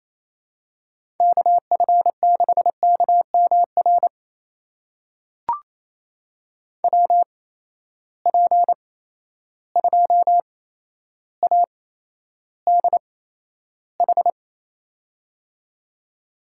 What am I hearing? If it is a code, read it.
Morse code, heard as a keyed tone, about 28 wpm, spaced as for 12 wpm, 700 Hz. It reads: KF6KMR WP2ADH
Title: USA Call Signs - 6 Characters - 8x Character Spacing - 28wpm